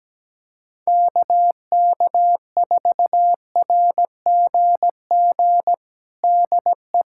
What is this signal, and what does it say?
Morse code, heard as a keyed tone, about 17 wpm, 700 Hz. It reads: KK4RGG DE
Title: USA Callsign DE Callsign 17wpm